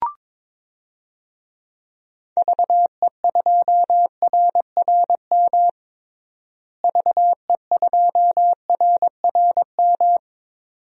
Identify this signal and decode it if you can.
Morse code, heard as a keyed tone, about 21 wpm, 700 Hz. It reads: VE2RRM VE2RRM
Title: DXCC-100 Call Signs - 6 Characters - Repeated 2x - 22wpm